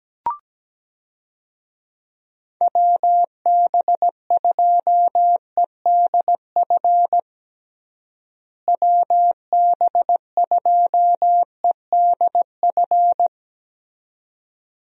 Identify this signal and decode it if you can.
Morse code, heard as a keyed tone, about 17 wpm, 700 Hz. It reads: WB2EDF WB2EDF
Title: USA Call Signs - 6 Characters - Repeated 2x - 17wpm